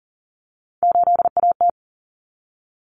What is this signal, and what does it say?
Morse code, heard as a keyed tone, about 40 wpm, 700 Hz. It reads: 8AT
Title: Common Cut Numbers - 3-Digit 40wpm